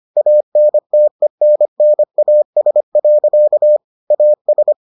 Morse code, heard as a keyed tone, about 25 wpm, 600 Hz. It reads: ANTENNAS. AS